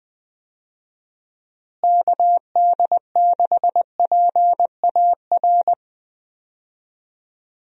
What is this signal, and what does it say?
Morse code, heard as a keyed tone, about 20 wpm, 700 Hz. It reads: KD6PAR
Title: USA Call Signs - 6 Characters - Repeated 3x - 20wpm